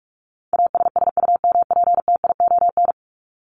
Morse code, heard as a keyed tone, about 44 wpm, 700 Hz. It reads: ASSUMPTION
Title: Top 1751-2000 Words 45wpm